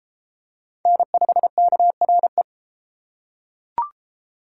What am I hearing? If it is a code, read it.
Morse code, heard as a keyed tone, about 31 wpm, 700 Hz. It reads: N5KRE